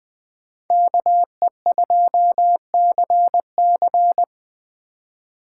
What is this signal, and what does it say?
Morse code, heard as a keyed tone, about 20 wpm, 700 Hz. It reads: KE2CC